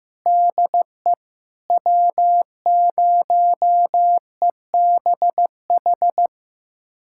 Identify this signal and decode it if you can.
Morse code, heard as a keyed tone, about 15 wpm, 700 Hz. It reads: DE W0EBH
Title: USA Callsign DE Callsign - 5 Characters - 15wpm